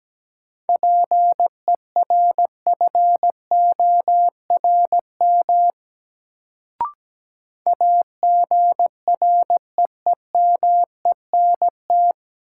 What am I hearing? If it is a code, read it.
Morse code, heard as a keyed tone, about 17 wpm, 700 Hz. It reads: PERFORM AGREEMENT
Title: Sentences from Top 1500 Words - New Word Drill Alt - 17wpm